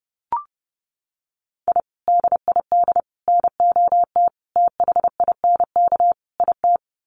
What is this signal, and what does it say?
Morse code, heard as a keyed tone, about 30 wpm, 700 Hz. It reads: I DID NOT THINK IT
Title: Sets of 5 Words Encore2 30wpm